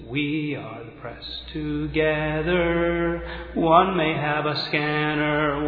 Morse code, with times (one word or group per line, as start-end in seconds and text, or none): none